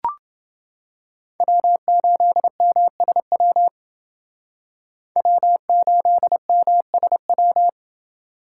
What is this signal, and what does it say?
Morse code, heard as a keyed tone, about 29 wpm, 700 Hz. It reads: W8MSW W8MSW